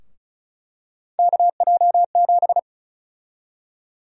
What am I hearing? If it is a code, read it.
Morse code, heard as a keyed tone, about 36 wpm, 700 Hz. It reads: KJ7
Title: USA Call Sign Prefixes - 35wpm